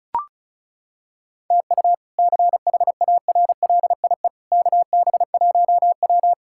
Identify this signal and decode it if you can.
Morse code, heard as a keyed tone, about 34 wpm, 700 Hz. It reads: TU CHARLIE KB1W
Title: K1USN SST QSO Practice - 35wpm